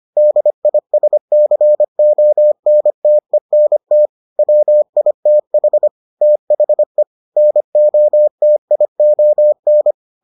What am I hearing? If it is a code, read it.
Morse code, heard as a keyed tone, about 25 wpm, 600 Hz. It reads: DISCONTENT WITH THE NOTION